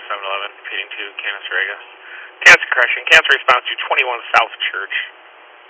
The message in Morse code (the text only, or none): none